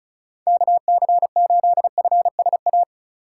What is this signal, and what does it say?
Morse code, heard as a keyed tone, about 36 wpm, 700 Hz. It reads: KC8FSA
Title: USA Call Signs - 6 Characters - 3x Character Spacing - 35wpm